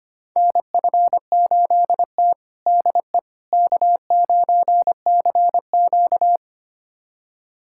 Morse code, heard as a keyed tone, about 25 wpm, 700 Hz. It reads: NF8T DE K9CQ
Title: USA Callsign DE Callsign - 4 Characters - 25wpm